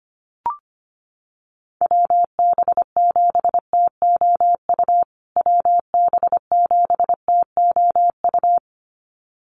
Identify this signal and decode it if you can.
Morse code, heard as a keyed tone, about 25 wpm, 700 Hz. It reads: WB7TOU WB7TOU
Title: USA Call Signs Repeated 2x 25wpm